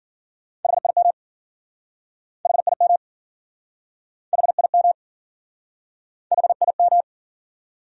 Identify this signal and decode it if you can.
Morse code, heard as a keyed tone, about 53 wpm, 700 Hz. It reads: HIM HIM HIM HIM